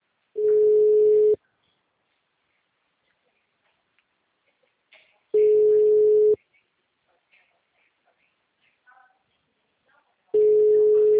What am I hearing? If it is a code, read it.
Ringback tone (Europe, 425 Hz).
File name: out- -2021- - - .12664